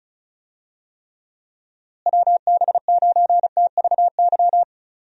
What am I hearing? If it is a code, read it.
Morse code, heard as a keyed tone, about 34 wpm, 700 Hz. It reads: WB9TVY
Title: USA Call Signs - 6 Characters - Repeated 2x - 35wpm